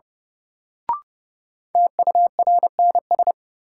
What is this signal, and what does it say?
Morse code, heard as a keyed tone, about 32 wpm, 700 Hz. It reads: TURNS
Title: Sets of 5 Words Encore - New Word Drill - 30wpm